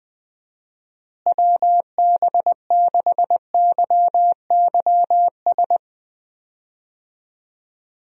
Morse code, heard as a keyed tone, about 20 wpm, 700 Hz. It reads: WB6YYS
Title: USA Call Signs - 6 Characters - Repeated 2x - 20wpm